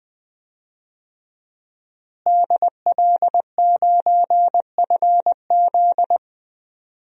Morse code, heard as a keyed tone, about 21 wpm, 700 Hz. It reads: DL9FZ
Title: DXCC-100 Call Signs - 5 Characters - Repeated 3x - 20wpm